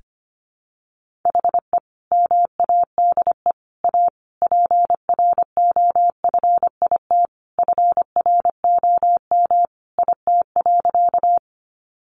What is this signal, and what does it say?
Morse code, heard as a keyed tone, about 25 wpm, 700 Hz. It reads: HE MADE A PROFIT FROM IT.
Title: Sentences from Top 800 Words 25wpm